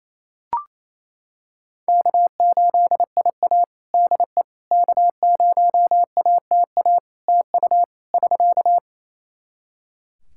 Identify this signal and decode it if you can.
Morse code, heard as a keyed tone, about 27 wpm, 700 Hz. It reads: K8IA DE K0ATA TU <SK>
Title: Practice QSO - Exchange 4 - Ending - 28wpm